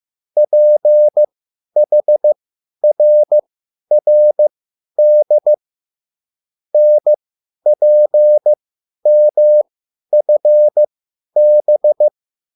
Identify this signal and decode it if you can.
Morse code, heard as a keyed tone, about 15 wpm, spaced as for 10 wpm, 600 Hz. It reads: PHRRD NPMFB
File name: QSL / CW lesson